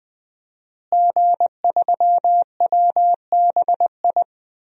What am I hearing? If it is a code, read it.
Morse code, heard as a keyed tone, about 20 wpm, 700 Hz. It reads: G3WBI
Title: DXCC-100 Call Signs - 5 Characters - Repeated 3x - 20wpm